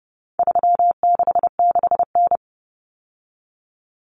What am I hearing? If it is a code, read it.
Morse code, heard as a keyed tone, about 30 wpm, 700 Hz. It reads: 366N
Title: Common Cut Numbers - 4-Digit 30wpm